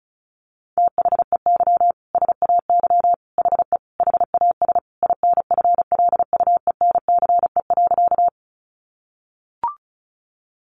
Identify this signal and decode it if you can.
Morse code, heard as a keyed tone, about 34 wpm, 700 Hz. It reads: THEY SAY HE HAS INFLUENCE.
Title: Sentences from Top 1100 Words 35wpm